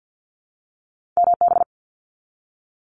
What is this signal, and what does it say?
Morse code, heard as a keyed tone, about 50 wpm, 700 Hz. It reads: MB
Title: Canadian Provinces and Territories 50wpm